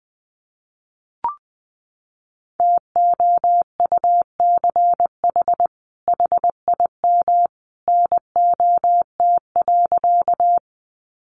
Morse code, heard as a keyed tone, about 20 wpm, 700 Hz. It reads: TOUCH HIM NOT.
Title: Sentences from Top 1200 Words 20wpm